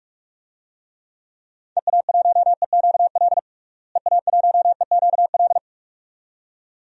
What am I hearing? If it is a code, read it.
Morse code, heard as a keyed tone, about 45 wpm, 700 Hz. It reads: EA1EQL EA1EQL
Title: DXCC-100 Call Signs - 6 Characters - Repeated 2x - 45wpm